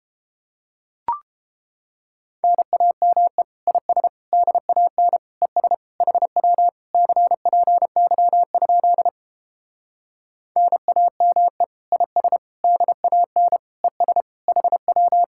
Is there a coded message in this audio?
Yes: Morse code from 2.4 to 15.2 s, heard as a keyed tone, about 31 wpm, 700 Hz. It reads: NAME IS DAN ES HW CPY? NAME IS DAN ES HW